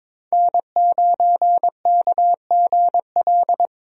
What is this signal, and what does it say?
Morse code, heard as a keyed tone, about 22 wpm, 700 Hz. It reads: N9KGL
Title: USA Call Signs - 5 Characters - 22wpm